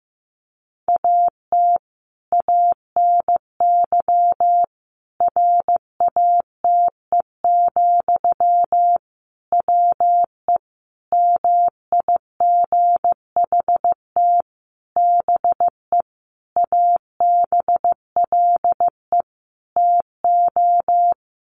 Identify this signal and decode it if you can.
Morse code, heard as a keyed tone, about 15 wpm, 700 Hz. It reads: AT ANY RATE, WE MIGHT BE ABLE TO
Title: Sentences from Top 700 Words 15wpm